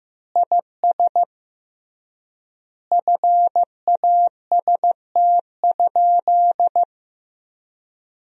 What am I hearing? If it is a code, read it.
Morse code, heard as a keyed tone, about 15 wpm, 700 Hz. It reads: IS FAST?
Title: 3x Word-Spacing - Sentences from Top 400 Words - 15wpm